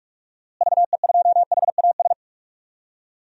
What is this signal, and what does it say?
Morse code, heard as a keyed tone, about 44 wpm, 700 Hz. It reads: VE2HAS
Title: DXCC-100 Call Signs - 6 Characters - Repeated 3x - 45wpm